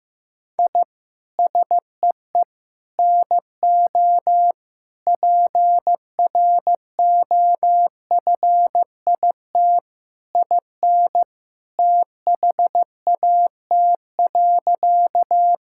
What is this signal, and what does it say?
Morse code, heard as a keyed tone, about 15 wpm, 700 Hz. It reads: I SEE NO PROFIT IN THAT.
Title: Sentences from Top 1000 Words - Review Part 5 - 15wpm